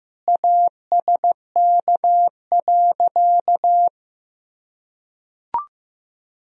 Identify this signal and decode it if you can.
Morse code, heard as a keyed tone, about 15 wpm, 700 Hz. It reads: ASK.
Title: Sentences from Top 1000 Words - Review Part 1 - 15wpm